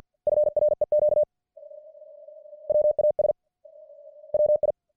Morse code, heard as a keyed tone, about 50 wpm, 600 Hz. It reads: FREQ WAS WI